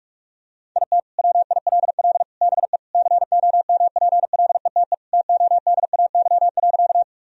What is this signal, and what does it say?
Morse code, heard as a keyed tone, about 46 wpm, 700 Hz. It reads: IT WILL BE COMPLETE TODAY.
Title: Sentences from Top 1000 Words - Review Part 2 - 45wpm